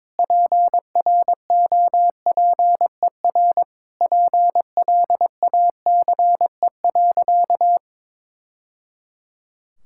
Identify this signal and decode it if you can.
Morse code, heard as a keyed tone, about 22 wpm, 700 Hz. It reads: PROPER PLACE.
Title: Sentences from Top 1000 Words - Review Part 5 - 22wpm